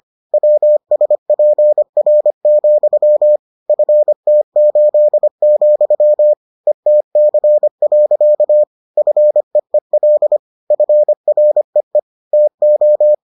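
Morse code, heard as a keyed tone, about 25 wpm, 600 Hz. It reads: WSPR, FT8, ETC. FEEL FREE TO
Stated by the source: Morse edition of the Foundations of Amateur Radio episodes.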